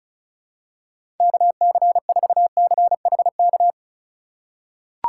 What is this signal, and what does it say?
Morse code, heard as a keyed tone, about 36 wpm, 700 Hz. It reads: KC4CHK
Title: USA Call Signs - 6 Characters - 2x Character Spacing - 35wpm